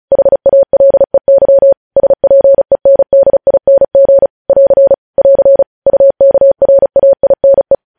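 Morse code, heard as a keyed tone, about 36 wpm, 550 Hz. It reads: HALEY SPENDING <AR> <AR> UKRAINE
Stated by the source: Morse practice sending news headlines at 35 wpm.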